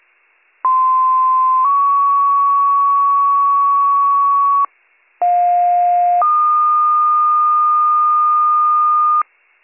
Two-tone page, heard as a one-second tone, then a three-second tone, about 1040 then 1120 Hz, about 710 then 1220 Hz. It is heard from 0.6 to 9.2 s.